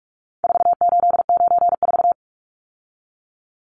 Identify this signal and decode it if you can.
Morse code, heard as a keyed tone, about 44 wpm, 700 Hz. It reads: 4894
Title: Numbers - 4-Digit 45wpm